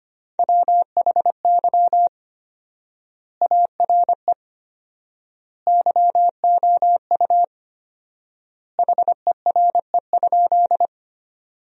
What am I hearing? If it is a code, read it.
Morse code, heard as a keyed tone, about 25 wpm, 700 Hz. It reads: WHY ARE YOU HERE?
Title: 4x Word-Spacing - Sentences from Top 200 Words - 25wpm